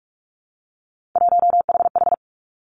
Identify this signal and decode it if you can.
Morse code, heard as a keyed tone, about 46 wpm, 700 Hz. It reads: 1HH
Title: Three Challenging Letter-Number Combinations 45wpm